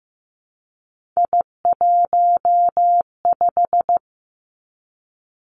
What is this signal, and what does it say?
Morse code, heard as a keyed tone, about 15 wpm, 700 Hz. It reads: I15
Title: Three Challenging Letter-Number Combinations 15wpm